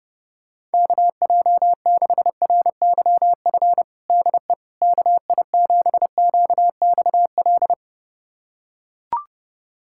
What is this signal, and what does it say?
Morse code, heard as a keyed tone, about 30 wpm, 700 Hz. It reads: KJ6RYF DE KI7QXL
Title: USA Callsign DE Callsign - 6 Characters - 30wpm